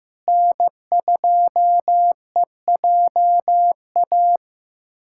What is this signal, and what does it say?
Morse code, heard as a keyed tone, about 15 wpm, 700 Hz. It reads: N2EJA
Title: USA Callsign DE Callsign 15wpm